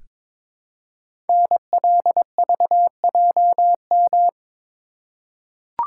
Morse code, heard as a keyed tone, about 23 wpm, 700 Hz. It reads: NLVJM
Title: Code Groups - 5 Letters - 22wpm